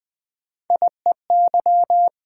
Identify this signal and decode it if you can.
Morse code, heard as a keyed tone, about 20 wpm, 700 Hz. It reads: IEY
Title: Three Letter-Number Combinations - 20wpm